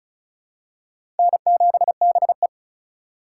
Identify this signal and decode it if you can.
Morse code, heard as a keyed tone, about 36 wpm, 700 Hz. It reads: N7BE
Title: USA Call Signs - 4 Characters - 4x Character Spacing - 35wpm